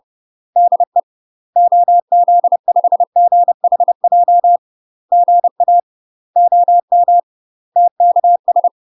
Morse code, heard as a keyed tone, about 30 wpm, 700 Hz. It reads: DE OZ5GHJ GA OM TKS
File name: UK MASTER QSOs TRAINING / 30WPM / UK MASTER QSOs 0048